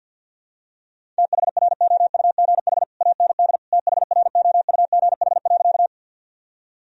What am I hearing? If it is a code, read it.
Morse code, heard as a keyed tone, about 46 wpm, 700 Hz. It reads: THROUGH AND THROUGH.
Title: Sentences from Top 1000 Words - Review Part 1 - 50wpm